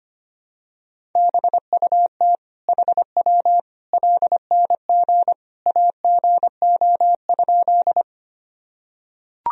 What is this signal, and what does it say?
Morse code, heard as a keyed tone, about 25 wpm, 700 Hz. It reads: BUT HW LNG AGO?
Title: Sentences from Top 500 Words and Abbreviations 25wpm